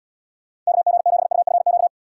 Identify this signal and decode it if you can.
Morse code, heard as a keyed tone, about 76 wpm, 700 Hz. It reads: KG7UFQ